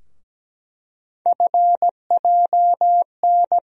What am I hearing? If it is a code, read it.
Morse code, heard as a keyed tone, about 17 wpm, 700 Hz. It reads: FJN
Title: USA Call Sign Postfixes - 3 Letters 17wpm